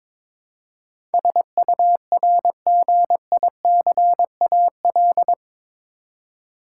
Morse code, heard as a keyed tone, about 22 wpm, 700 Hz. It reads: SURGICAL
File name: practice / 22wpm / Eight Letter Words - 22wpm